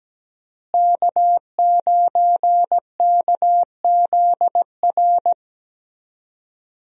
Morse code, heard as a keyed tone, about 17 wpm, 700 Hz. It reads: K9KZR
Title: USA Call Signs - 5 Characters - Repeated 2x - 17wpm